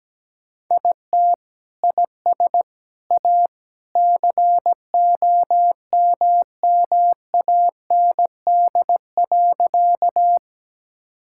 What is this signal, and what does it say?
Morse code, heard as a keyed tone, about 17 wpm, 700 Hz. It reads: IT IS A COMMAND.
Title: Sentences from Top 1600 Words 17wpm